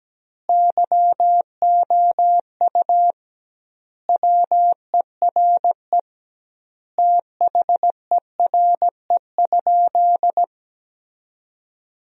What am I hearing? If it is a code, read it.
Morse code, heard as a keyed tone, about 17 wpm, 700 Hz. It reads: YOU WERE THERE?
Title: 2x Word-Spacing - Sentences from Top 500 Words - 17wpm